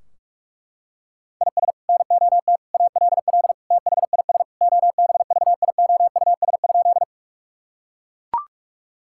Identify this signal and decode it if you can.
Morse code, heard as a keyed tone, about 41 wpm, 700 Hz. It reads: IS NOT ALL THIS OBVIOUS?
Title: Sentences from Top 1600 Words 45wpm